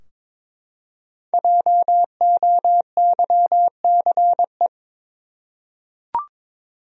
Morse code, heard as a keyed tone, about 22 wpm, 700 Hz. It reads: JOYCE
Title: USA Common Names - Max 5 Letters - 22wpm